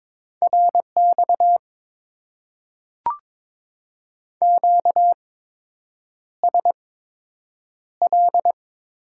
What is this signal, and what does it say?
Morse code, heard as a keyed tone, about 22 wpm, spaced as for 8 wpm, 690 Hz. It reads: RX QSL
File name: Top 90 QSO Elements - 8x Character Spacing - 22wpm